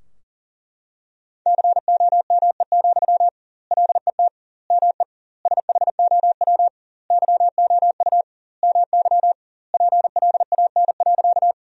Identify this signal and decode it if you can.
Morse code, heard as a keyed tone, about 40 wpm, 700 Hz. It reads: COME, LET ME SHOW YOU MY PLAN.
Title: 2x Word-Spacing - Sentences from Top 500 Words - 40wpm